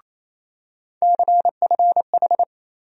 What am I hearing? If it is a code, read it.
Morse code, heard as a keyed tone, about 28 wpm, 700 Hz. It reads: CFH
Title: USA Call Sign Postfixes - 3 Letters 28wpm